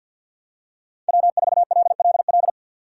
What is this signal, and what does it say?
Morse code, heard as a keyed tone, about 47 wpm, 700 Hz. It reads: W4LLL
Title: USA Call Signs - 5 Characters - 4x Character Spacing - 50wpm